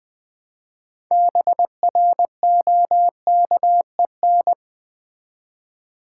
Morse code, heard as a keyed tone, about 20 wpm, 700 Hz. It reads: BROKEN
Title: Build A Word - 3 Letters At A Time - 6-Letter Words - 20wpm